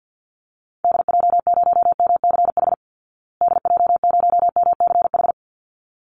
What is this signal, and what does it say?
Morse code, heard as a keyed tone, about 50 wpm, 700 Hz. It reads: DJ0MCH DJ0MCH
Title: Call Signs Repeated 3x - 50wpm